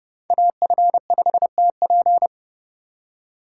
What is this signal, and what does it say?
Morse code, heard as a keyed tone, about 31 wpm, 700 Hz. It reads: AF5TP
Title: USA Call Signs - 5 Characters - 4x Character Spacing - 30wpm